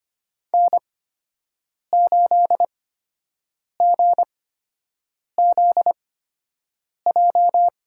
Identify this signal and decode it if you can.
Morse code, heard as a keyed tone, about 25 wpm, spaced as for 7 wpm, 700 Hz. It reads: N8GZJ
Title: USA Call Signs - 5 Characters - 8x Character Spacing - 25wpm